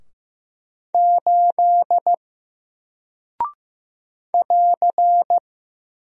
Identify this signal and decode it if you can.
Morse code, heard as a keyed tone, about 15 wpm, 700 Hz. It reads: 8 <AR>
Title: Single Character (Letter:Number:Punctuation:Prosign) 15wpm